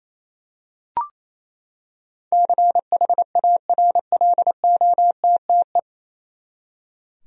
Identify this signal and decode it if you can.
Morse code, heard as a keyed tone, about 28 wpm, 700 Hz. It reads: CHARLOTTE
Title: USA Common Names - 28wpm